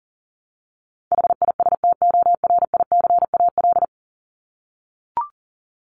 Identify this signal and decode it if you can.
Morse code, heard as a keyed tone, about 40 wpm, 700 Hz. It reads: HISTORICAL
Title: Top 501-750 Words 40wpm